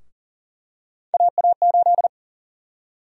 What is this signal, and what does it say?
Morse code, heard as a keyed tone, about 40 wpm, 700 Hz. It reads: AA8
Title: USA Call Sign Prefixes - Repeated 2x - 40wpm